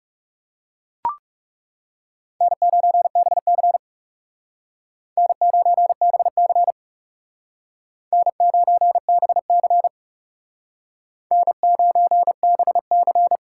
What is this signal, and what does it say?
Morse code, heard as a keyed tone, about 38 wpm, 700 Hz. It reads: N9BC N9BC N9BC N9BC